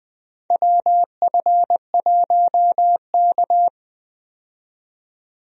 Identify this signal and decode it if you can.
Morse code, heard as a keyed tone, about 20 wpm, 700 Hz. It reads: WF1K
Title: USA Call Signs - 4 Characters - 20wpm